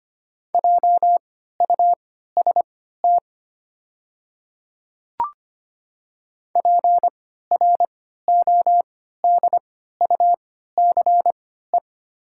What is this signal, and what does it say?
Morse code, heard as a keyed tone, about 25 wpm, spaced as for 14 wpm, 700 Hz. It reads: JUST PRODUCE
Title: 3x Character Spacing - Top 500 Words - 25wpm